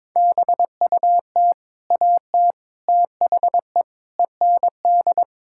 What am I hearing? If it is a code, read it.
Morse code, heard as a keyed tone, about 22 wpm, 700 Hz. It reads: BUT AT THE END
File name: practice / 22wpm / Sets of 8 Words - 22wpm